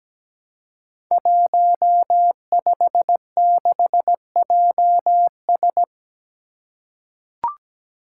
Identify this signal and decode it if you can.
Morse code, heard as a keyed tone, about 17 wpm, 700 Hz. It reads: 156JS
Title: Five Challenging Letter-Number Combinations 17wpm